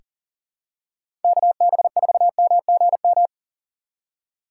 Morse code, heard as a keyed tone, about 40 wpm, 700 Hz. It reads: KB4MGM
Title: USA Call Signs - 2x Character Spacing - 40wpm